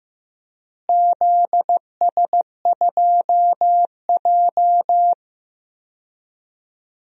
Morse code, heard as a keyed tone, about 15 wpm, 700 Hz. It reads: ZS2J
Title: DXCC-100 Call Signs - 4 Characters - Repeated 3x - 15wpm